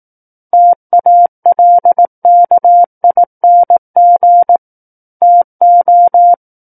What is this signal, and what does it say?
Morse code, heard as a keyed tone, about 18 wpm, 700 Hz. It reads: TALKING TO
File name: 18 WPM